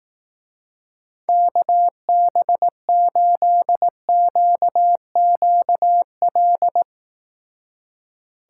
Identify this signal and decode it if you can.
Morse code, heard as a keyed tone, about 18 wpm, 700 Hz. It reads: KB8QQL
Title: Speed Racing - USA 2x3 Call Signs 15wpm